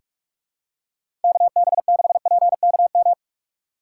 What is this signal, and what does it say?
Morse code, heard as a keyed tone, about 46 wpm, 700 Hz. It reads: KB6PKM
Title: USA Call Signs - 2x Character Spacing - 45wpm